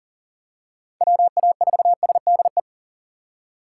Morse code, heard as a keyed tone, about 43 wpm, 700 Hz. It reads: WA4SDE
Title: USA Call Signs - 6 Characters - 6x Character Spacing - 40wpm